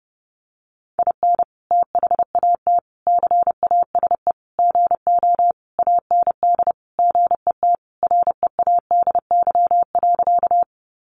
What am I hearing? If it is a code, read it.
Morse code, heard as a keyed tone, about 30 wpm, 700 Hz. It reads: IN THAT CASE GO AND GET READY.
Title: Sentences from Top 700 Words 30wpm